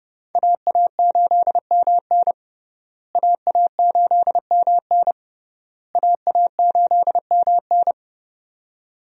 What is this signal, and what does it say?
Morse code, heard as a keyed tone, about 32 wpm, 700 Hz. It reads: AA8MN AA8MN AA8MN
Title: USA Call Signs - 5 Characters - Repeated 3x - 30wpm